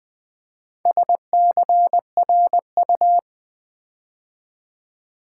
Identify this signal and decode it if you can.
Morse code, heard as a keyed tone, about 20 wpm, 700 Hz. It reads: SCRU
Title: Build A Word - 4 Letters At A Time - 8-Letter Words - 20wpm